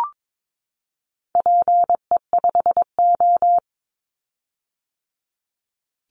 Morse code, heard as a keyed tone, about 22 wpm, 700 Hz. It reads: PE5O
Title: Call Signs - 22wpm